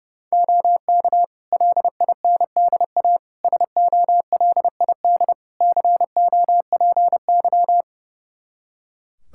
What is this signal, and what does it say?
Morse code, heard as a keyed tone, about 30 wpm, 700 Hz. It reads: OK LINDA SOLID COPY
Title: Practice QSO - Exchange 3 - More Info - 30wpm